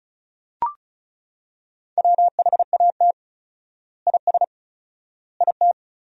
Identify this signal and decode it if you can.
Morse code, heard as a keyed tone, about 36 wpm, 700 Hz. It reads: WHAT IS IT
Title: 4x Word-Spacing - Sentences from Top 100 Words - 35wpm